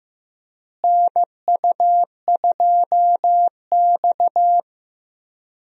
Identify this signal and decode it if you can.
Morse code, heard as a keyed tone, about 16 wpm, 700 Hz. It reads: NU2X
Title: USA Call Signs - 4 Characters - Repeated 3x - 15wpm